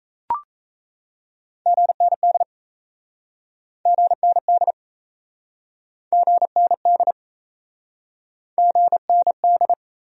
Morse code, heard as a keyed tone, about 38 wpm, 700 Hz. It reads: GND GND GND GND